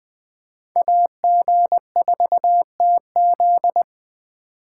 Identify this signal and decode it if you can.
Morse code, heard as a keyed tone, about 20 wpm, 700 Hz. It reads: AG4TZ
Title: Speed Racing - USA 2x2 Call Signs 17wpm